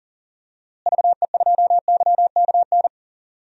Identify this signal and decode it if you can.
Morse code, heard as a keyed tone, about 37 wpm, 700 Hz. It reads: VE2YKN
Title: DXCC-100 Call Signs - 6 Characters - Repeated 3x - 40wpm